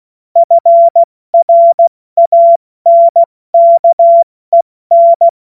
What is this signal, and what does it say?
Morse code, heard as a keyed tone, about 16 wpm, 680 Hz. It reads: FRANKEN
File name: Morsekurs / Witze / Witz 07. Juni 2022 70 F80